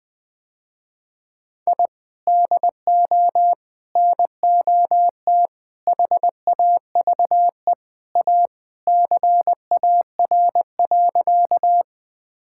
Morse code, heard as a keyed tone, about 20 wpm, 700 Hz. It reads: I DO NOT HAVE A CAR.
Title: Sentences with 5-Word Phrases - Spoken Beginning - 20wpm